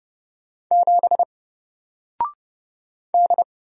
Morse code, heard as a keyed tone, about 30 wpm, 700 Hz. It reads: 7 D
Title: Single Character (Letter:Number:Punctuation:Prosign) 30wpm